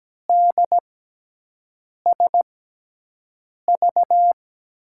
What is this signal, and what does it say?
Morse code, heard as a keyed tone, about 17 wpm, spaced as for 6 wpm, 700 Hz. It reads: DSV
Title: USA Call Signs - 6x Character Spacing - 17wpm